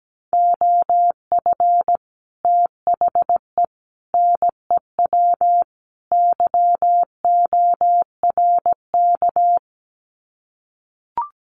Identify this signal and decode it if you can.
Morse code, heard as a keyed tone, about 17 wpm, 700 Hz. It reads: OF THE NEW YORK
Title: Sets of 4 Words 17wpm